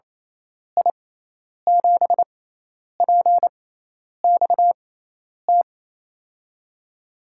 Morse code, heard as a keyed tone, about 28 wpm, spaced as for 10 wpm, 700 Hz. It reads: I7PXT
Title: USA Call Signs - 6 Characters - 6x Character Spacing - 28wpm